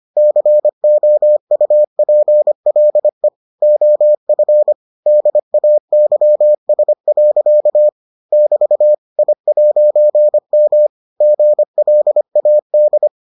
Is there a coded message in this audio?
Yes: Morse code from 0.2 to 13.1 s, heard as a keyed tone, about 25 wpm, 600 Hz. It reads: COUPLE OF DAYS. <BT> I'M GLAD